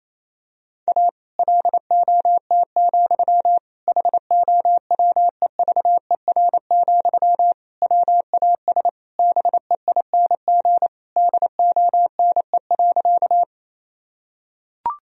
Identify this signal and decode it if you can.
Morse code, heard as a keyed tone, about 28 wpm, 700 Hz. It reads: A LOT, HOWEVER, WAS BEING DONE.
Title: Sentences from Top 1000 Words - Review Part 3 - 28wpm